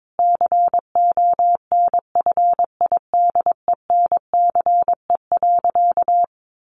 Morse code, heard as a keyed tone, about 22 wpm, 700 Hz. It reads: CONFIDENCE.
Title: Sentences from Top 1500 Words 22wpm